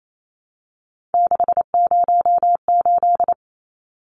Morse code, heard as a keyed tone, about 28 wpm, 700 Hz. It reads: 608
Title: Numbers - 3-Digit 28wpm